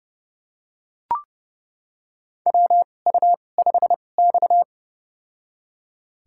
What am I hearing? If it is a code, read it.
Morse code, heard as a keyed tone, about 30 wpm, spaced as for 22 wpm, 700 Hz. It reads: WU5X
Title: USA Call Signs - 2x Character Spacing - 30wpm